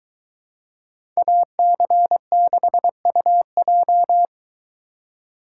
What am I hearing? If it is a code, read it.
Morse code, heard as a keyed tone, about 23 wpm, 700 Hz. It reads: AC6UJ